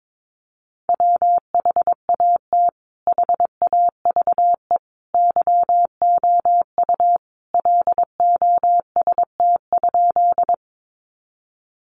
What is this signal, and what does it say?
Morse code, heard as a keyed tone, about 22 wpm, 700 Hz. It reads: WHAT HAVE YOU LOST?
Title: Sentences from Top 1400 Words 22wpm